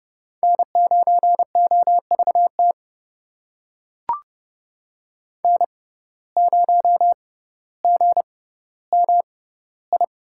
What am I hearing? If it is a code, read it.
Morse code, heard as a keyed tone, about 30 wpm, spaced as for 15 wpm, 700 Hz. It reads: N9OVT N0GMI
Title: USA Call Signs - 5 Characters - 6x Character Spacing - 30wpm